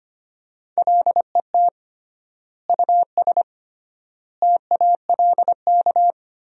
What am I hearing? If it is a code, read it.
Morse code, heard as a keyed tone, about 25 wpm, 700 Hz. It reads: LET US TALK
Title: 3x Word-Spacing - Sentences from Top 400 Words - 25wpm